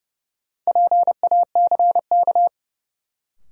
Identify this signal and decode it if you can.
Morse code, heard as a keyed tone, about 30 wpm, 700 Hz. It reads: PACK
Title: Sentences from Top 1300 Words - New Word Drill - 30wpm